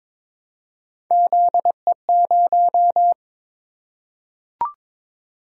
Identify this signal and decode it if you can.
Morse code, heard as a keyed tone, about 22 wpm, 700 Hz. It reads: ZE0
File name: practice / 22wpm / Three Letter-Number Combinations - 22wpm